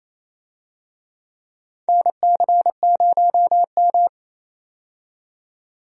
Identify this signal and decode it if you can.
Morse code, heard as a keyed tone, about 28 wpm, 700 Hz. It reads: NC0M
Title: USA Call Signs - 4 Characters - 28wpm